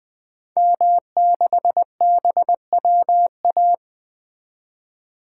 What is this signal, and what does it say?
Morse code, heard as a keyed tone, about 20 wpm, 700 Hz. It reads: M6BWA
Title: 2023 SOTA Activator Call Signs - 20wpm